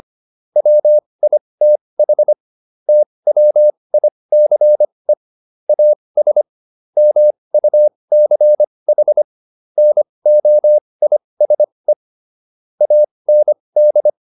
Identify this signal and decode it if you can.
Morse code, heard as a keyed tone, about 25 wpm, spaced as for 20 wpm, 600 Hz. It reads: WITH TWICE AS MUCH NOISE AND